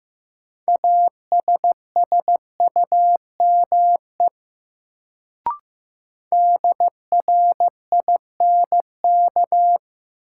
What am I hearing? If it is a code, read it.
Morse code, heard as a keyed tone, about 15 wpm, 700 Hz. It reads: ASSUME DRINK
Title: Sentences from Top 1500 Words - New Word Drill - 15wpm